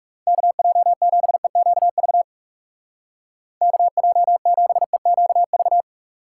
Morse code, heard as a keyed tone, about 38 wpm, 700 Hz. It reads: KJ7EQV KJ7EQV